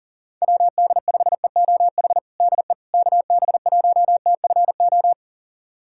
Morse code, heard as a keyed tone, about 40 wpm, 700 Hz. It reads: WD5EOH DE KB1TFO
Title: USA Callsign DE Callsign - 6 Characters - 40wpm